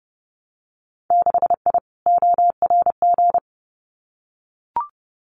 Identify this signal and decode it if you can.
Morse code, heard as a keyed tone, about 30 wpm, 700 Hz. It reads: 6I ORG
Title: Winter Field Day Exchanges Encore 30wpm